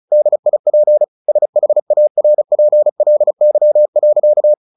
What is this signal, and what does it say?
Morse code, heard as a keyed tone, about 34 wpm, 600 Hz. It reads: DIP SHARPLY.
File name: the old man and the sea / the old man and the sea 1 35wpm